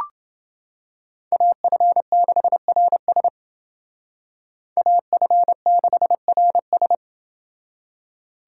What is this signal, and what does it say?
Morse code, heard as a keyed tone, about 29 wpm, 700 Hz. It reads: AF6RS AF6RS